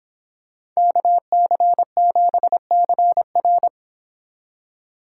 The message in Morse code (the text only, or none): KC7CR